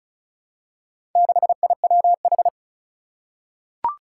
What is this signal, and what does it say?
Morse code, heard as a keyed tone, about 36 wpm, 700 Hz. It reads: 6IWH
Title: Four Challenging Letter-Number Combinations 35wpm